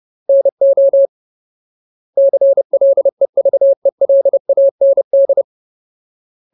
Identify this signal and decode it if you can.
Morse code, heard as a keyed tone, about 30 wpm, 550 Hz. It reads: NO CLEVELAND
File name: Ultimate Word List / 30WPM Ultimate